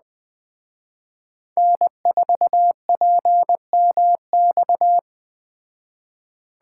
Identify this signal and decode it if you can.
Morse code, heard as a keyed tone, about 20 wpm, 700 Hz. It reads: N4PMX